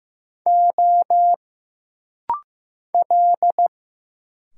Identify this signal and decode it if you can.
Morse code, heard as a keyed tone, about 15 wpm, 700 Hz. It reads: O L
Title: Single Letters - Rapid-Fire 15wpm